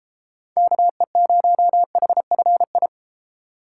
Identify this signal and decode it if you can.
Morse code, heard as a keyed tone, about 32 wpm, 700 Hz. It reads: KE0HFI